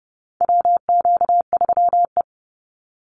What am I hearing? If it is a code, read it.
Morse code, heard as a keyed tone, about 30 wpm, 700 Hz. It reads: WQ3E
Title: Call Signs Repeated 3x - 30wpm